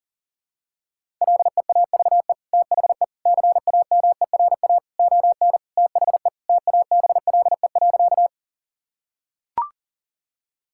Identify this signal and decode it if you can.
Morse code, heard as a keyed tone, about 37 wpm, 700 Hz. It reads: LEAVE THE CAMERA ON THE TABLE.
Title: Sentences from Top 1000 Words - Review Part 5 - 40wpm